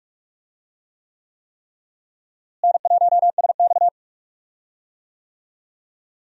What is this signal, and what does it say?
Morse code, heard as a keyed tone, about 46 wpm, 700 Hz. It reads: N1SX
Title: USA Call Signs - 4 Characters - Repeated 3x - 45wpm